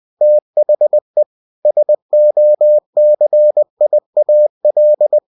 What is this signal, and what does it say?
Morse code, heard as a keyed tone, about 20 wpm, 600 Hz. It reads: THE SOCIAL
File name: cw-books / winnie-the-pooh / winnie-20wpm-page-0109